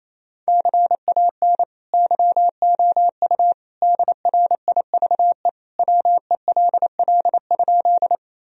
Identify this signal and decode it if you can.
Morse code, heard as a keyed tone, about 28 wpm, 700 Hz. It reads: CAN YOU DRIVE WELL?
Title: Sentences from Top 1000 Words - Review Part 2 - 28wpm